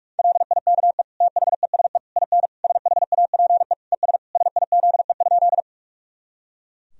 Morse code, heard as a keyed tone, about 46 wpm, 700 Hz. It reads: LIKE THESE IN SHAPE ES SIZE?
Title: Sentences from Top 500 Words and Abbreviations 45wpm